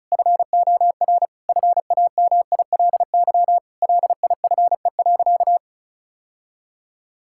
Morse code, heard as a keyed tone, about 34 wpm, 700 Hz. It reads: FOR FAMILY LIFE.
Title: Sentences from Top 1000 Words - Review Part 5 - 35wpm